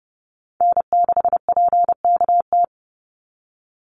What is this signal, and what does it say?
Morse code, heard as a keyed tone, about 29 wpm, 700 Hz. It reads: N6PKT
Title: NA SOTA Chaser Callsigns 30wpm